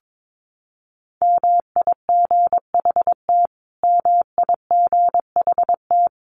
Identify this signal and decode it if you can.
Morse code, heard as a keyed tone, about 22 wpm, 700 Hz. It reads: MIGHT MIGHT
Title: Top 500 Words Repeated 22wpm